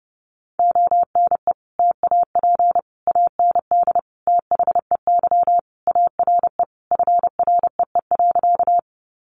Morse code, heard as a keyed tone, about 30 wpm, 700 Hz. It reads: ONE TAP AND THEY ARE FREE.
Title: Sentences from Top 1400 Words 30wpm